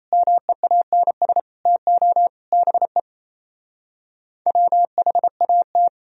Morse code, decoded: MEANS TO BE WHAT

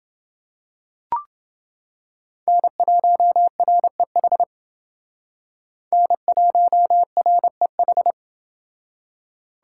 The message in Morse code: N1REH N1REH

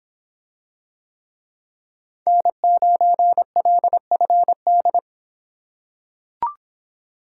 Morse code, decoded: N9LFD

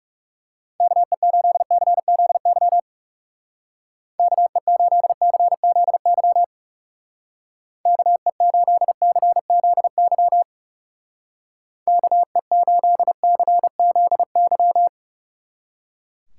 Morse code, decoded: KE8CZY KE8CZY KE8CZY KE8CZY

